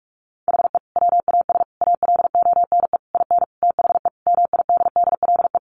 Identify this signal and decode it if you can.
Morse code, heard as a keyed tone, about 44 wpm, 700 Hz. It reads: HE WAS ALONE IN THE MIDDLE
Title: Sentences from Top 1000 Words 45wpm